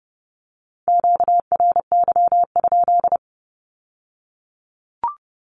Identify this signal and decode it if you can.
Morse code, heard as a keyed tone, about 30 wpm, 700 Hz. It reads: QRY?
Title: Q-Codes with Spoken Meaning - 30wpm